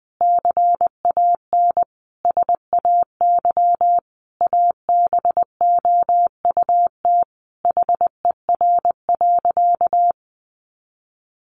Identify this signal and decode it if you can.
Morse code, heard as a keyed tone, about 20 wpm, 700 Hz. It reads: CAN SAY ABOUT HER.
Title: Sentences from Top 200 Words 20wpm